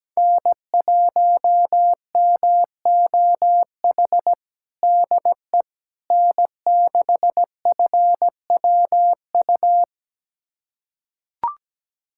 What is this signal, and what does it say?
Morse code, heard as a keyed tone, about 17 wpm, 700 Hz. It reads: N1MOH DE N6FWU
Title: USA Callsign DE Callsign - 5 Characters - 17wpm